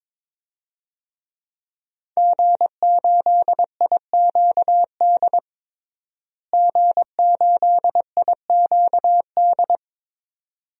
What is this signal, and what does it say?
Morse code, heard as a keyed tone, about 22 wpm, 700 Hz. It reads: G8IQD G8IQD
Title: DXCC-100 Call Signs - 5 Characters - Repeated 2x - 22wpm